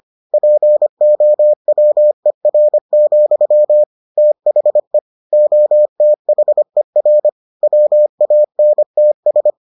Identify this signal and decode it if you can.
Morse code, heard as a keyed tone, about 25 wpm, 600 Hz. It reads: POWER, THE OTHER WANTS